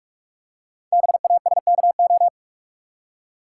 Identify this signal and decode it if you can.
Morse code, heard as a keyed tone, about 44 wpm, 690 Hz. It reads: BASKO
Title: Code Groups - 5 Letters - 45wpm